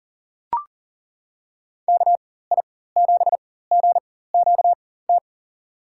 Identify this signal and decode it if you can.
Morse code, heard as a keyed tone, about 40 wpm, spaced as for 19 wpm, 700 Hz. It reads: KI7GQT
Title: USA Call Signs - 6 Characters - 4x Character Spacing - 40wpm